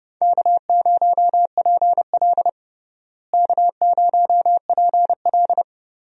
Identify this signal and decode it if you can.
Morse code, heard as a keyed tone, about 29 wpm, 700 Hz. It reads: K0PL K0PL
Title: USA Call Signs - 4 Characters - Repeated 3x - 30wpm